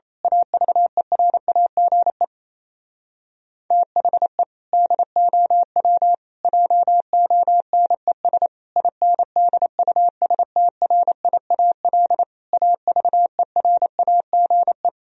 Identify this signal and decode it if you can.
Morse code, heard as a keyed tone, about 26 wpm, 700 Hz. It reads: AVERAGE THE DOW JONES INDUSTRIAL AVERAGE
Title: Speed-Racing - Sets of 5 Words 28wpm